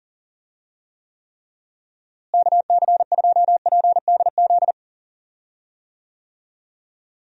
Morse code, heard as a keyed tone, about 40 wpm, 700 Hz. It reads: KC2PDZ
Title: USA Call Signs - 6 Characters - Repeated 3x - 40wpm